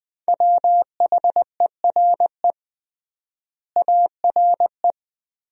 Morse code, heard as a keyed tone, about 20 wpm, 700 Hz. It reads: WHERE ARE
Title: 3x Word-Spacing - Sentences from Top 200 Words - 20wpm